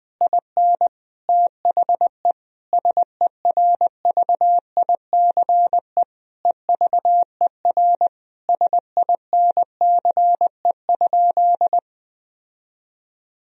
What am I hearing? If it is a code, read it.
Morse code, heard as a keyed tone, about 20 wpm, 700 Hz. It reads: IN THE SERVICE EVER SINCE?
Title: Sentences from Top 1000 Words - Review Part 4 - 20wpm